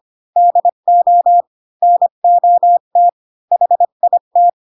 Morse code, heard as a keyed tone, about 25 wpm, 700 Hz. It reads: DO NOT HIT
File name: Pinocchio / Pinocchio ch1 20 25 ik1hge